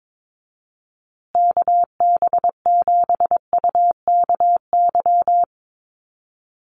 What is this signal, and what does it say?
Morse code, heard as a keyed tone, about 22 wpm, 700 Hz. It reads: KB7UKY
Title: USA Call Signs Repeated 2x 22wpm